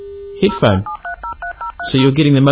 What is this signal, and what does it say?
DTMF, heard as telephone touch tones, tones of about 110 ms, with speech. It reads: 07303030